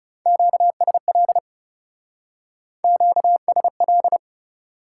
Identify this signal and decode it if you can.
Morse code, heard as a keyed tone, about 35 wpm, 700 Hz. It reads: QSL QSL
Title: Speed-Racing Top 180 QSO Elements 30wpm